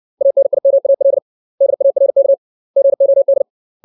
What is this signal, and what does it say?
Morse code, heard as a keyed tone, about 60 wpm, 550 Hz. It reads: ANIMAL DARK GOD
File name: 632 words / 632 Words 60WPM